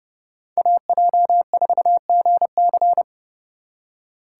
Morse code, heard as a keyed tone, about 40 wpm, 700 Hz. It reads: AJ4GC